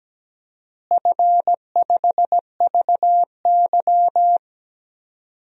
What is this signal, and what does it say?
Morse code, heard as a keyed tone, about 17 wpm, 700 Hz. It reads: F5VY
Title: DXCC-100 Call Signs - 4 Characters - Repeated 3x - 17wpm